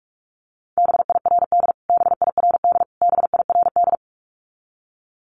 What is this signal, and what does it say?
Morse code, heard as a keyed tone, about 44 wpm, 700 Hz. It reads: BIRD BIRD BIRD
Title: Top 500 Words Repeated 45wpm